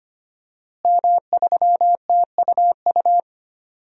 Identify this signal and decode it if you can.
Morse code, heard as a keyed tone, about 25 wpm, 700 Hz. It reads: M3TUU
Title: DXCC-100 Call Signs - 5 Characters - Repeated 2x - 25wpm